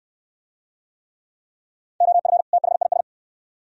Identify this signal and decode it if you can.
Morse code, heard as a keyed tone, about 69 wpm, 700 Hz. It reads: OF THIS